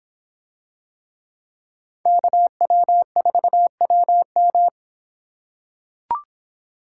Morse code, heard as a keyed tone, about 25 wpm, 700 Hz. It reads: KW4WM